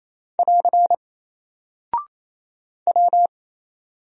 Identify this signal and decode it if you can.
Morse code, heard as a keyed tone, about 27 wpm, 700 Hz. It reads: <AR> W
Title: Single Character (Letter:Number:Punctuation:Prosign) 28wpm